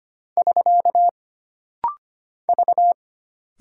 Morse code, heard as a keyed tone, about 25 wpm, 700 Hz. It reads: <SK> V
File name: Single Character (Letter:Number:Punctuation:Prosign) - Rapid-Fire - 25wpm